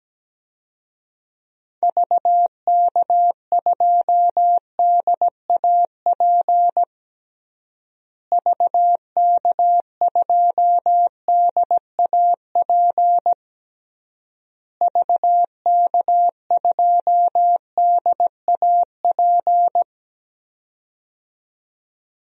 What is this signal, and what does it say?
Morse code, heard as a keyed tone, about 17 wpm, 700 Hz. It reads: VK2DAP VK2DAP VK2DAP
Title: DXCC-100 Call Signs - 6 Characters - Repeated 3x - 17wpm